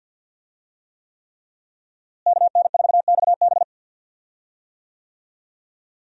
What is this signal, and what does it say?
Morse code, heard as a keyed tone, about 52 wpm, 700 Hz. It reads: KN4XB
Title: USA Call Signs - 5 Characters - Repeated 3x - 50wpm